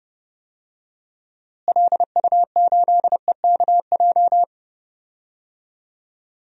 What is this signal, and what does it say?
Morse code, heard as a keyed tone, about 31 wpm, 700 Hz. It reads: LU8EKJ
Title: DXCC-100 Call Signs - 6 Characters - Repeated 3x - 30wpm